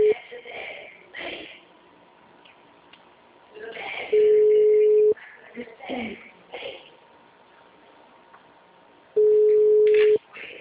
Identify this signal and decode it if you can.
Ringback tone (Europe, 425 Hz), with speech.